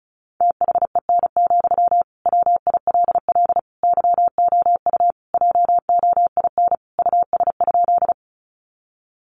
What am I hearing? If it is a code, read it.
Morse code, heard as a keyed tone, about 34 wpm, 700 Hz. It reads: THEN, WILL YOU JOIN US?
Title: Sentences from Top 1400 Words 35wpm